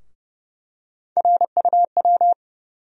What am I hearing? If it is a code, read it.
Morse code, heard as a keyed tone, about 30 wpm, 700 Hz. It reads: RUW
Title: USA Call Sign Postfixes - 3 Letters 30wpm